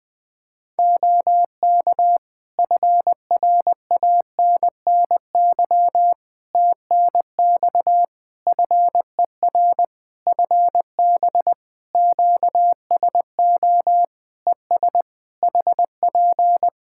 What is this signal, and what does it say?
Morse code, heard as a keyed tone, about 20 wpm, 700 Hz. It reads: OK FRANNY TNX FER FB QSO ES HP
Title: Practice QSO - Exchange 4 - Ending - 20wpm